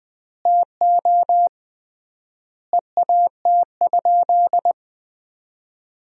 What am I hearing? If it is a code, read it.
Morse code, heard as a keyed tone, about 20 wpm, 700 Hz. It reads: TO EAT?
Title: 3x Word-Spacing - Sentences from Top 300 Words - 20wpm